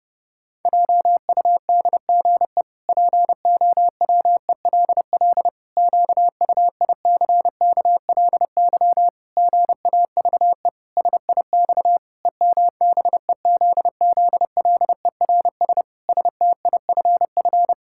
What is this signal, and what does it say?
Morse code, heard as a keyed tone, about 30 wpm, 700 Hz. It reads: JUDGE POWELL QUICKLY GAVE SIX EMBEZZLERS STIFF
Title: Pangram Challenge v2 - 30wpm